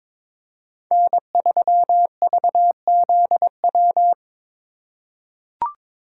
Morse code, heard as a keyed tone, about 22 wpm, 700 Hz. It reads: N3VZW
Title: USA Call Signs - 5 Characters - 4x Character Spacing - 22wpm